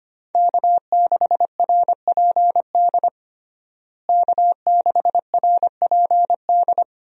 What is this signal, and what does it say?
Morse code, heard as a keyed tone, about 25 wpm, 700 Hz. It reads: K6RPD K6RPD
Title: USA Call Signs - 5 Characters - Repeated 2x - 25wpm